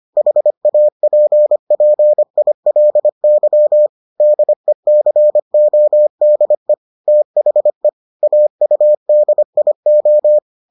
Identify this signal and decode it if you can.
Morse code, heard as a keyed tone, about 25 wpm, 600 Hz. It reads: HAPPILY DECODE THE AUDIO